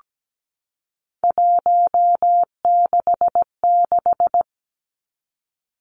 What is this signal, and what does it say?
Morse code, heard as a keyed tone, about 17 wpm, 700 Hz. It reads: 166
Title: Three Challenging Letter-Number Combinations 17wpm